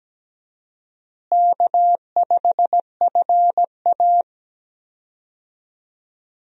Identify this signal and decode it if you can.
Morse code, heard as a keyed tone, about 17 wpm, 700 Hz. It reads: K5FA
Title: USA Call Signs - 4 Characters - Repeated 2x - 17wpm